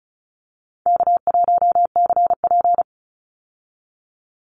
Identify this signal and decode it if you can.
Morse code, heard as a keyed tone, about 36 wpm, 700 Hz. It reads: K1CP
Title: Call Signs - 35wpm